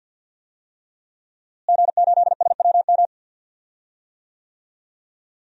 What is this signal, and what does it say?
Morse code, heard as a keyed tone, about 52 wpm, 700 Hz. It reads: G8SWM
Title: DXCC-100 Call Signs - 5 Characters - Repeated 3x - 50wpm